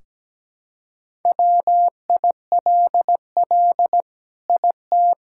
Morse code, heard as a keyed tone, about 17 wpm, 700 Hz. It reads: WILL IT
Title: Sentences from Top 1600 Words 17wpm